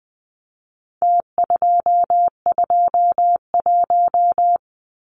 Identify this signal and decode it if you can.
Morse code, heard as a keyed tone, about 20 wpm, 700 Hz. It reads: T221
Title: Numbers - Contest Style - 4-Digit 20wpm